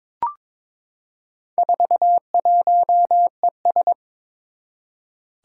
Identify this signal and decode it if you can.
Morse code, heard as a keyed tone, about 22 wpm, 700 Hz. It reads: 41ES
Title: Four Challenging Letter-Number Combinations 22wpm